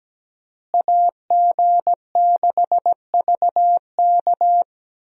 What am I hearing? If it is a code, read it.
Morse code, heard as a keyed tone, about 17 wpm, 700 Hz. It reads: AG6VK
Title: USA Call Signs - 5 Characters - Repeated 3x - 17wpm